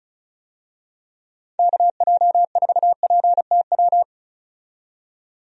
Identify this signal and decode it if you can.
Morse code, heard as a keyed tone, about 36 wpm, 700 Hz. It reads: KJ4PTW
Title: USA Call Signs - 6 Characters - 35wpm